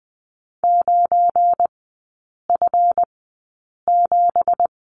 Morse code, heard as a keyed tone, about 20 wpm, spaced as for 8 wpm, 700 Hz. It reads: 9F7
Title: 198 Lesson 32e - Letter k - Three Characters 20wpm